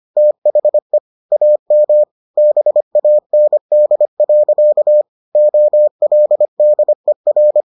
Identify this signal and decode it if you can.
Morse code, heard as a keyed tone, about 25 wpm, 600 Hz. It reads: THE AM BAND. OLDER